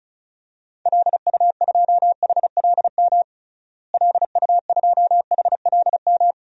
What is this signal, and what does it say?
Morse code, heard as a keyed tone, about 34 wpm, 700 Hz. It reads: LU2HLM LU2HLM
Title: DXCC-100 Call Signs - 6 Characters - Repeated 3x - 35wpm